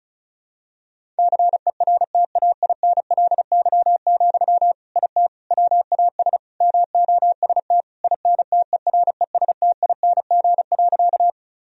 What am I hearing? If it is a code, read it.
Morse code, heard as a keyed tone, about 36 wpm, 700 Hz. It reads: CERTAINLY, IT WAS MOST INTERESTING.
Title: Sentences from Top 1000 Words - Review Part 1 - 35wpm